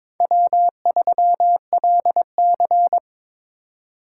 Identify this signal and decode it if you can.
Morse code, heard as a keyed tone, about 22 wpm, 700 Hz. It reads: W3LC
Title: USA Call Signs - 4 Characters - 4x Character Spacing - 22wpm